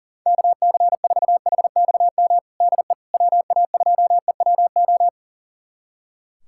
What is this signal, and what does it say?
Morse code, heard as a keyed tone, about 40 wpm, 700 Hz. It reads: KC4HXM DE WA2EWO
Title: USA Callsign DE Callsign 40wpm